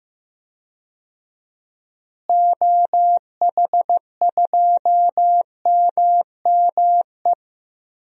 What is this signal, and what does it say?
Morse code, heard as a keyed tone, about 15 wpm, 700 Hz. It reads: OH2MME
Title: DXCC-100 Call Signs - 6 Characters - Repeated 3x - 15wpm